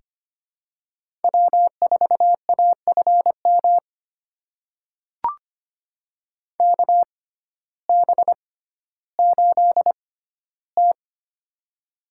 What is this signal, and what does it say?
Morse code, heard as a keyed tone, about 25 wpm, spaced as for 14 wpm, 700 Hz. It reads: W4AFM KB8T
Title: USA Call Signs - 6x Character Spacing - 25wpm